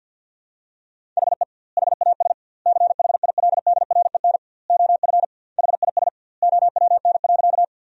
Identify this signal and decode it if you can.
Morse code, heard as a keyed tone, about 49 wpm, 700 Hz. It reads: HE HAS CHILDREN OF HIS OWN.
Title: 2x Word-Spacing - Sentences from Top 300 Words - 50wpm